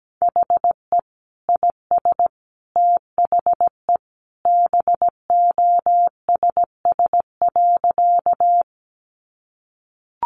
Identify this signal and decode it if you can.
Morse code, heard as a keyed tone, about 17 wpm, 700 Hz. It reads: HE IS THE BOSS.
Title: Sentences from Top 700 Words 17wpm